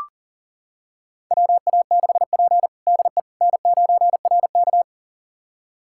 Morse code, heard as a keyed tone, about 40 wpm, 700 Hz. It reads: WA6P DE N9RK
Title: USA Callsign DE Callsign - 4 Characters - 40wpm